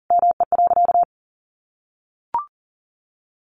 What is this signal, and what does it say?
Morse code, heard as a keyed tone, about 40 wpm, 700 Hz. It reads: ME.
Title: Sentences from Top 1100 Words 40wpm